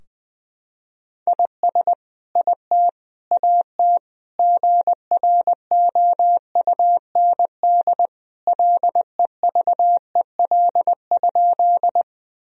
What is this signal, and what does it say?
Morse code, heard as a keyed tone, about 20 wpm, 700 Hz. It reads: IS IT AT GROUND LEVEL?
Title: Sentences from Top 1000 Words - Review Part 4 - 20wpm